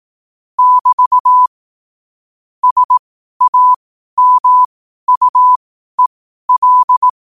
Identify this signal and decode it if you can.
Morse code, heard as a keyed tone, about 18 wpm, spaced as for 12 wpm, 1000 Hz. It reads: <BT> SAMUEL
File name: qotdfiles / 12wpm